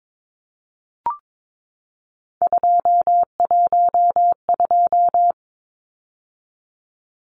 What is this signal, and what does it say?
Morse code, heard as a keyed tone, about 22 wpm, 700 Hz. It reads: 212
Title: Numbers - 3-Digit 22wpm